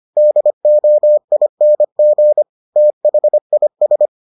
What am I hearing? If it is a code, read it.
Morse code, heard as a keyed tone, about 25 wpm, 600 Hz. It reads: DOING THIS